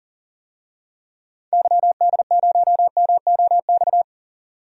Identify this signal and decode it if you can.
Morse code, heard as a keyed tone, about 37 wpm, 700 Hz. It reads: YD0MOX
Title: DXCC-100 Call Signs - 6 Characters - 40wpm